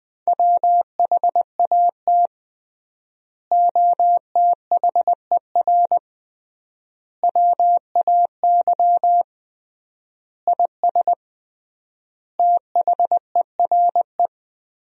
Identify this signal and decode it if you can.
Morse code, heard as a keyed tone, about 19 wpm, 700 Hz. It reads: WHAT OTHER WAY IS THERE
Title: 3x Word-Spacing - Sentences from Top 100 Words - 20wpm